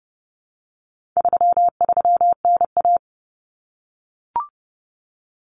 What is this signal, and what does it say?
Morse code, heard as a keyed tone, about 30 wpm, 700 Hz. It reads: 33NA
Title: Common Cut Numbers - 4-Digit 30wpm